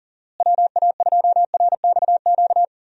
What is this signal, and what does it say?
Morse code, heard as a keyed tone, about 40 wpm, 700 Hz. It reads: WA2RXQ
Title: USA Call Signs - 6 Characters - Repeated 2x - 40wpm